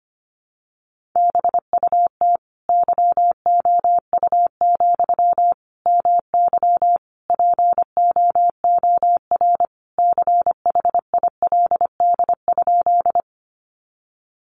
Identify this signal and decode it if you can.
Morse code, heard as a keyed tone, about 25 wpm, 700 Hz. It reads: BUT YOU, MY POOR CHILD?
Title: Sentences from Top 1000 Words 25wpm